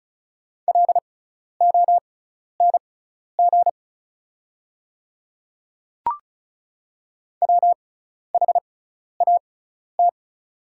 Morse code, heard as a keyed tone, about 36 wpm, spaced as for 12 wpm, 700 Hz. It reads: LONG WHAT
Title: 6x Character Spacing - Top 100 Words - 35wpm